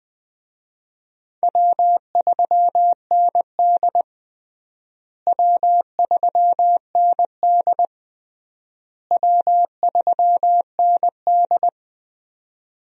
Morse code, heard as a keyed tone, about 19 wpm, 700 Hz. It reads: W3ND W3ND W3ND
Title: USA Call Signs - 4 Characters - Repeated 3x - 20wpm